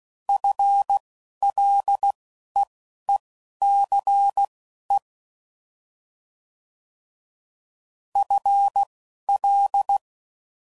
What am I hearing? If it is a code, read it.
Morse code, heard as a keyed tone, about 16 wpm, spaced as for 11 wpm, 800 Hz. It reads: FLEECE FL
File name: Prefixes / FLs & THs / FL Prefixes 3